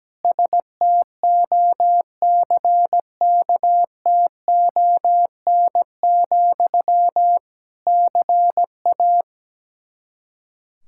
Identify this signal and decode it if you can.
Morse code, heard as a keyed tone, about 17 wpm, 700 Hz. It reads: STOCKTON, CA
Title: K1USN SST QSO Practice 17wpm